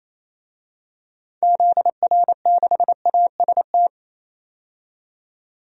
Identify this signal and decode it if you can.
Morse code, heard as a keyed tone, about 28 wpm, 700 Hz. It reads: ZR6AST
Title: DXCC-100 Call Signs - 6 Characters - 28wpm